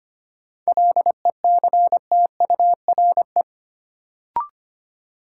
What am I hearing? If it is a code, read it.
Morse code, heard as a keyed tone, about 25 wpm, 700 Hz. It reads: LECTURE
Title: Sentences from Top 1600 Words - New Word Drill Alt - 25wpm